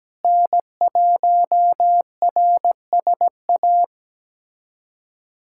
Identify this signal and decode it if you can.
Morse code, heard as a keyed tone, about 17 wpm, 700 Hz. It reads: N1RSA
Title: USA Callsign DE Callsign - 5 Characters - 17wpm